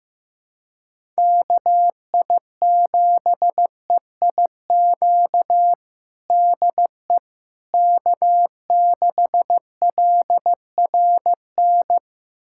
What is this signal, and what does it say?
Morse code, heard as a keyed tone, about 15 wpm, 700 Hz. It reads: KI7EIQ DE K6LRN
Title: USA Callsign DE Callsign 15wpm